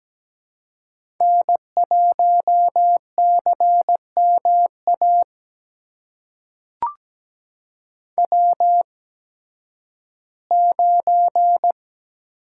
Morse code, heard as a keyed tone, about 17 wpm, spaced as for 11 wpm, 700 Hz. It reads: N1CMA W9
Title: USA Call Signs - 8x Character Spacing - 17wpm